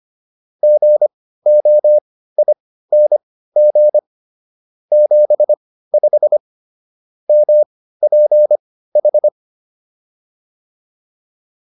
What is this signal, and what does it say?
Morse code, heard as a keyed tone, about 25 wpm, spaced as for 15 wpm, 600 Hz. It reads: GOING 75 MPH